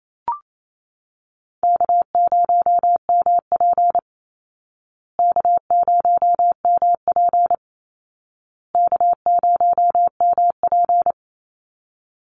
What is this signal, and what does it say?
Morse code, heard as a keyed tone, about 28 wpm, 700 Hz. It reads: K0MP K0MP K0MP
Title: Call Signs Repeated 2x - 28wpm